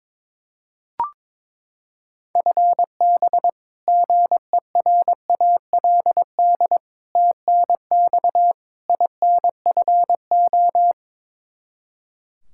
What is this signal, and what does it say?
Morse code, heard as a keyed tone, about 21 wpm, 700 Hz. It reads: FB GERALD TNX INFO
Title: Practice QSO - Exchange 3 - More Info - 22wpm